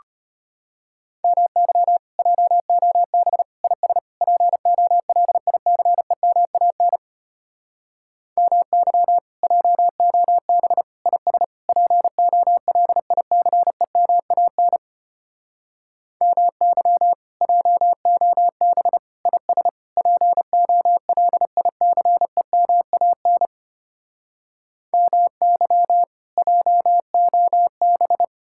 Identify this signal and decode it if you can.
Morse code, heard as a keyed tone, about 33 wpm, 700 Hz. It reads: MY JOB IS POLICEMAN MY JOB IS POLICEMAN MY JOB IS POLICEMAN MY JOB